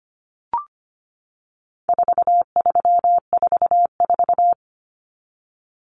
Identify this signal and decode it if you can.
Morse code, heard as a keyed tone, about 25 wpm, 700 Hz. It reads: 4344
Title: Common Cut Numbers - 4-Digit 25wpm